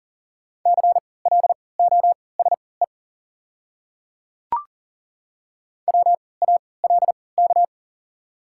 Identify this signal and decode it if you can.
Morse code, heard as a keyed tone, about 40 wpm, spaced as for 23 wpm, 700 Hz. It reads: CLOSE WALK
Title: 3x Character Spacing - Top 300 Words - 40wpm